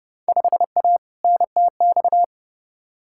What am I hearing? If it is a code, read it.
Morse code, heard as a keyed tone, about 30 wpm, 700 Hz. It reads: 5A NTX
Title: Field Day Exchanges - 30wpm